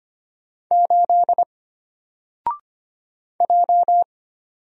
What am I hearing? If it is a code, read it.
Morse code, heard as a keyed tone, about 25 wpm, 700 Hz. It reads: 8 J
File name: Single-Letter-Number-25wpm